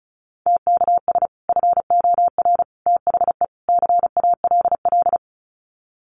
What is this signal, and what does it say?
Morse code, heard as a keyed tone, about 34 wpm, 700 Hz. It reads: TKS FOR THE CALL
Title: Common QSO Phrases 35wpm